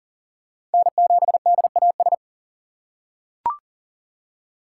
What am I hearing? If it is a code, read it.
Morse code, heard as a keyed tone, about 40 wpm, 700 Hz. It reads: N7DAS
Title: USA Call Signs - 5 Characters - 3x Character Spacing - 40wpm